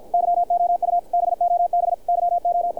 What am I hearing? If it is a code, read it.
Morse code, heard as a keyed tone, about 50 wpm, 700 Hz, with radio noise. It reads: YOU BOB YO
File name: archived / CRSnet